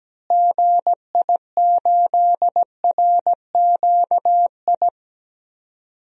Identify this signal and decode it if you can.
Morse code, heard as a keyed tone, about 17 wpm, 700 Hz. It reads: GI8RQI
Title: DXCC-100 Call Signs - 6 Characters - Repeated 2x - 17wpm